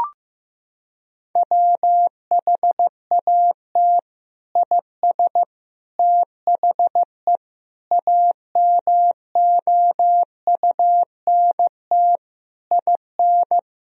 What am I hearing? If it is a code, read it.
Morse code, heard as a keyed tone, about 15 wpm, 700 Hz. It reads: WHAT IS THE AMOUNT IN
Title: Sentences from Top 1000 Words - Review Part 4 - 15wpm